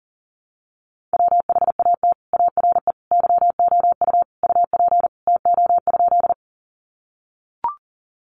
Morse code, heard as a keyed tone, about 40 wpm, 700 Hz. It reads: WHAT ARE YOU UP TO?
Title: Sentences from Top 100 Words 40wpm